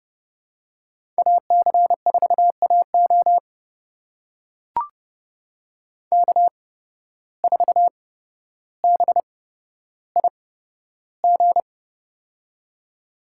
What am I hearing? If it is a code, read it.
Morse code, heard as a keyed tone, about 31 wpm, spaced as for 13 wpm, 700 Hz. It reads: AC4AO K4BIG
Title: USA Call Signs - 5 Characters - 8x Character Spacing - 30wpm